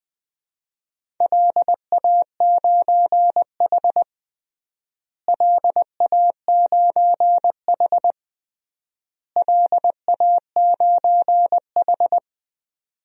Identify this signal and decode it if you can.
Morse code, heard as a keyed tone, about 20 wpm, 700 Hz. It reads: LA9H LA9H LA9H
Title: DXCC-100 Call Signs - 4 Characters - Repeated 3x - 20wpm